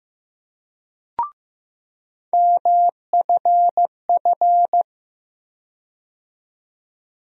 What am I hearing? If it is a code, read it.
Morse code, heard as a keyed tone, about 15 wpm, 700 Hz. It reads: MFF
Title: USA Call Sign Postfixes - 3 Letters 15wpm